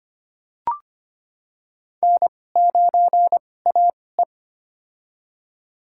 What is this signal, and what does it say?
Morse code, heard as a keyed tone, about 25 wpm, spaced as for 18 wpm, 700 Hz. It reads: N9AE
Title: USA Call Signs - 2x Character Spacing - 25wpm